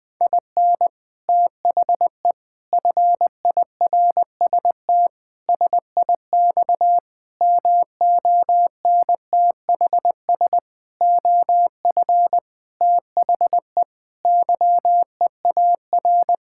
Morse code, heard as a keyed tone, about 20 wpm, 700 Hz. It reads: IN THE FIRST SIX MONTHS OF THE YEAR
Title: Sentences with 6-Word Phrases - Spoken Beginning - 20wpm